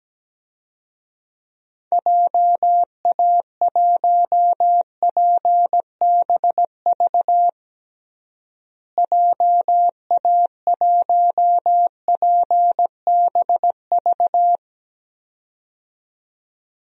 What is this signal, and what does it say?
Morse code, heard as a keyed tone, about 17 wpm, 700 Hz. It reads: JA1PBV JA1PBV
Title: DXCC-100 Call Signs - 6 Characters - Repeated 2x - 17wpm